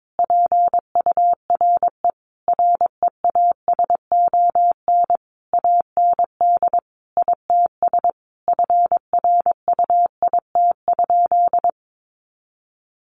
Morse code, heard as a keyed tone, about 22 wpm, 700 Hz. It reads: PURE REASON AND ITS FRUIT?
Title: Sentences from Top 1500 Words 22wpm